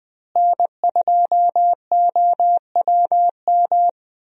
Morse code, heard as a keyed tone, about 20 wpm, 700 Hz. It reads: N2OWM
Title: USA Call Signs - 5 Characters - Repeated 3x - 20wpm